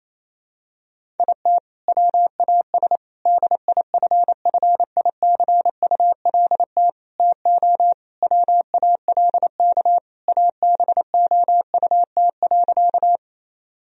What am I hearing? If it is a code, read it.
Morse code, heard as a keyed tone, about 28 wpm, 700 Hz. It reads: IT WAS DIFFICULT TO WALK ABOUT.
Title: Sentences from Top 1000 Words - Review Part 1 - 28wpm